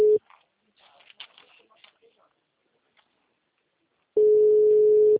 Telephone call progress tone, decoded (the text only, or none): ringback tone